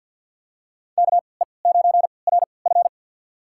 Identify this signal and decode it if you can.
Morse code, heard as a keyed tone, about 51 wpm, spaced as for 28 wpm, 700 Hz. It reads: KE9RF
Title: USA Call Signs - 3x Character Spacing - 50wpm